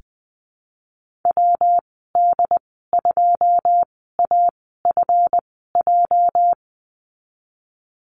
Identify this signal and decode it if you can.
Morse code, heard as a keyed tone, about 20 wpm, spaced as for 15 wpm, 700 Hz. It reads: WD2AFJ
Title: 238 Lesson 38g - Letter q - Call Signs 20wpm